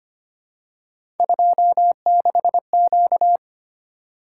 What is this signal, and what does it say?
Morse code, heard as a keyed tone, about 25 wpm, 700 Hz. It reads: 26Q
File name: Three Letter-Number Combinations - 25wpm